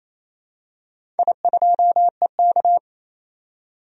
Morse code, heard as a keyed tone, about 28 wpm, 700 Hz. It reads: I2EK
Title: DXCC-100 Call Signs - 4 Characters - 28wpm